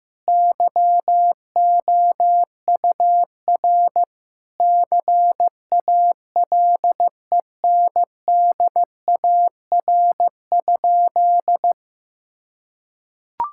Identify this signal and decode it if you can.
Morse code, heard as a keyed tone, about 15 wpm, 700 Hz. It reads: YOUR CALENDAR?
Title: Sentences from Top 1600 Words 15wpm